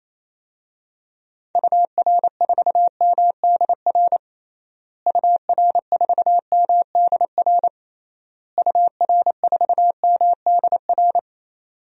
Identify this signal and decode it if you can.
Morse code, heard as a keyed tone, about 28 wpm, 700 Hz. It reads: UR4MDR UR4MDR UR4MDR
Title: DXCC-100 Call Signs - 6 Characters - Repeated 3x - 28wpm